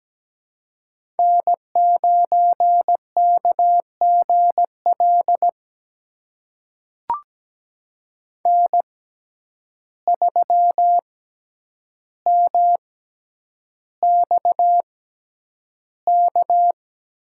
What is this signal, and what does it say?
Morse code, heard as a keyed tone, about 17 wpm, spaced as for 9 wpm, 700 Hz. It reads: N9KGL N3MXK
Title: USA Call Signs - 5 Characters - 6x Character Spacing - 17wpm